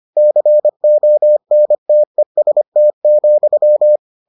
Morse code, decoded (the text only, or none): CONTEST,